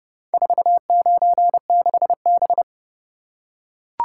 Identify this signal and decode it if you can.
Morse code, heard as a keyed tone, about 29 wpm, 700 Hz. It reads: 496B